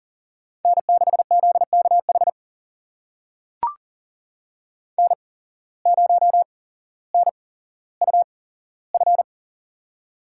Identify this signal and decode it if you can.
Morse code, heard as a keyed tone, about 39 wpm, spaced as for 17 wpm, 700 Hz. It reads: N6ZKH N0NUF
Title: USA Call Signs - 5 Characters - 8x Character Spacing - 40wpm